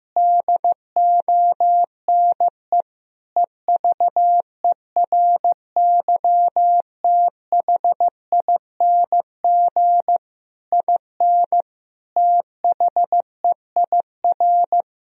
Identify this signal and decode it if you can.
Morse code, heard as a keyed tone, about 15 wpm, 700 Hz. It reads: DONE EVERYTHING IN THEIR
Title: Sentences from Top 1000 Words - Review Part 2 - 15wpm